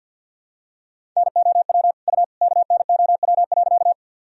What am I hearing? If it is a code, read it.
Morse code, heard as a keyed tone, about 48 wpm, 700 Hz. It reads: NOW U KNOW.
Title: Sentences from Top 100 Words and Abbreviations 50wpm